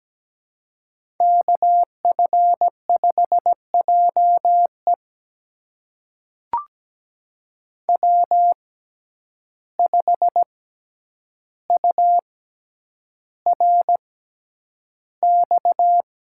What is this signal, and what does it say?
Morse code, heard as a keyed tone, about 17 wpm, spaced as for 9 wpm, 700 Hz. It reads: KF5JE W5URX
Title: USA Call Signs - 5 Characters - 6x Character Spacing - 17wpm